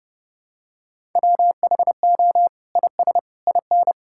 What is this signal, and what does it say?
Morse code, heard as a keyed tone, about 30 wpm, 700 Hz. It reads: WHO IS IN
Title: Sentences from Top 1000 Words - Review Part 6 - 30wpm